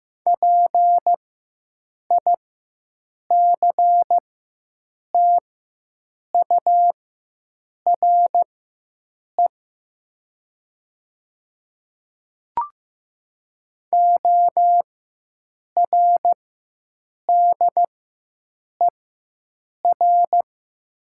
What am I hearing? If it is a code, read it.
Morse code, heard as a keyed tone, about 15 wpm, spaced as for 7 wpm, 690 Hz. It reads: PICTURE ORDER
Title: 4x Character Spacing - Top 500 Words - 15wpm